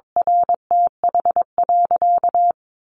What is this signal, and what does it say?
Morse code, heard as a keyed tone, about 22 wpm, 700 Hz. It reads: RTH.
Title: Sentences from Top 1200 Words 22wpm